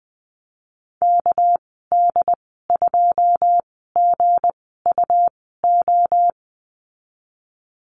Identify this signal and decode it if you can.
Morse code, heard as a keyed tone, about 20 wpm, 700 Hz. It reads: K D 2 G U O
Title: 200 Lesson 32g - Letter k - Call Signs 20wpm